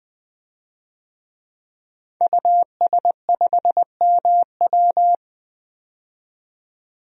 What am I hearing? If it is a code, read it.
Morse code, heard as a keyed tone, about 21 wpm, 700 Hz. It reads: US5MW
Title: DXCC-100 Call Signs - 5 Characters - 20wpm